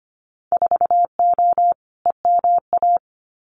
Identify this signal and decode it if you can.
Morse code, heard as a keyed tone, about 25 wpm, 700 Hz. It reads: 4O EMA
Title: Winter Field Day Exchanges 25wpm